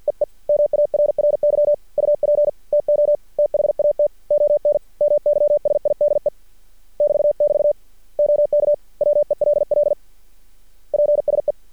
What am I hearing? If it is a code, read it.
Morse code, heard as a keyed tone, about 50 wpm, 590 Hz. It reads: EE MARRY UP TO THAT ON MY SIDE <BT><BT> OK WELL PSE
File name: archived / CRSnet